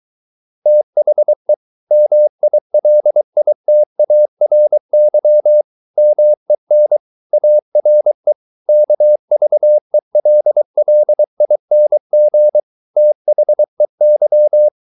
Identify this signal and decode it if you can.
Morse code, heard as a keyed tone, about 23 wpm, 600 Hz. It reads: THE MILITARY MEN ARE KVELLING THEY